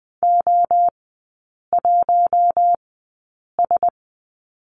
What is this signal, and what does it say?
Morse code, heard as a keyed tone, about 20 wpm, spaced as for 8 wpm, 700 Hz. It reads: O1S
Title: 053 Lesson 10e - Letter r - Three Characters 20wpm